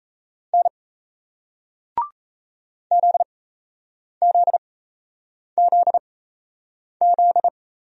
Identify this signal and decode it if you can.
Morse code, heard as a keyed tone, about 38 wpm, spaced as for 8 wpm, 700 Hz. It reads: N ZZZZ